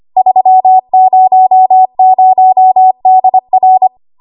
Morse code, heard as a keyed tone, about 25 wpm, 750 Hz. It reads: 300DR
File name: Archive / 25 WPM / 25WPM